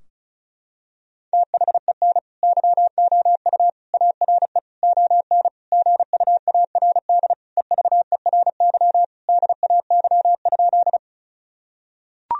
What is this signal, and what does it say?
Morse code, heard as a keyed tone, about 36 wpm, 700 Hz. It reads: THEN YOU ARE ON GUARD EVERY DAY?
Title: Sentences from Top 1000 Words - Review Part 4 - 35wpm